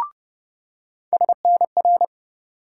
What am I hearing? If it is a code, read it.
Morse code, heard as a keyed tone, about 30 wpm, 700 Hz. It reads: SNR
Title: USA Call Sign Postfixes - 3 Letters 30wpm